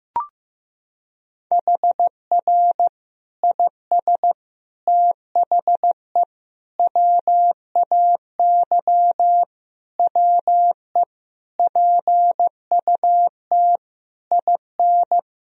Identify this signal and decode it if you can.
Morse code, heard as a keyed tone, about 15 wpm, 700 Hz. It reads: HR IS THE WAY WE PUT IN
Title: Sentences from Top 200 Words and Abbreviations 15wpm